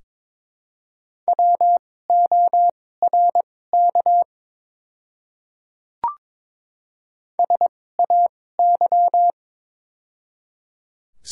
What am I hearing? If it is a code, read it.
Morse code, heard as a keyed tone, about 21 wpm, spaced as for 16 wpm, 700 Hz. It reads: WORK SAY
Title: 2x Character Spacing - Top 300 Words - 22wpm